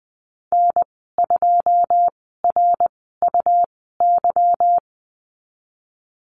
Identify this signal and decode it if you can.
Morse code, heard as a keyed tone, about 20 wpm, spaced as for 15 wpm, 700 Hz. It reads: N2RUY
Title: 167 Lesson 27g - Number 7 - Call Signs 20wpm